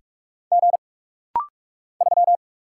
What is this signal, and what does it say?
Morse code, heard as a keyed tone, about 43 wpm, 700 Hz. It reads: G 3
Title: Single Letter-Number - Rapid-Fire 45wpm